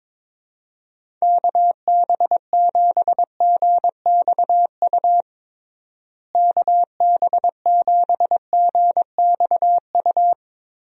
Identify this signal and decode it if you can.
Morse code, heard as a keyed tone, about 22 wpm, 700 Hz. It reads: KB7GXU KB7GXU
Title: USA Call Signs - 6 Characters - Repeated 3x - 22wpm